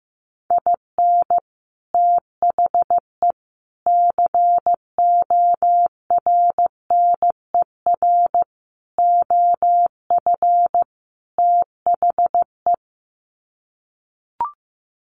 Morse code, heard as a keyed tone, about 15 wpm, 700 Hz. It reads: IN THE CORNER OF THE
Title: Sets of 5 Words Encore 15wpm